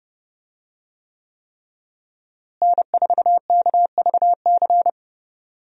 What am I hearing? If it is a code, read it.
Morse code, heard as a keyed tone, about 30 wpm, 700 Hz. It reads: N4KVC
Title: USA Call Signs - 5 Characters - Repeated 3x - 30wpm